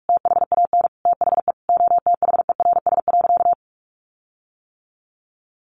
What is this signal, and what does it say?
Morse code, heard as a keyed tone, about 44 wpm, 700 Hz. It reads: THAN THE OTHERS.
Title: Sentences from Top 1500 Words 45wpm